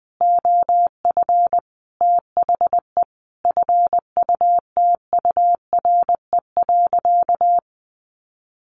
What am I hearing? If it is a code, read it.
Morse code, heard as a keyed tone, about 20 wpm, 700 Hz. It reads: OF THE FUTURE.
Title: Sentences from Top 700 Words 20wpm